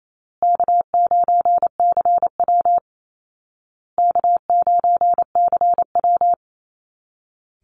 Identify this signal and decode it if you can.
Morse code, heard as a keyed tone, about 28 wpm, 700 Hz. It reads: K9CW K9CW
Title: Call Signs Repeated 2x - 28wpm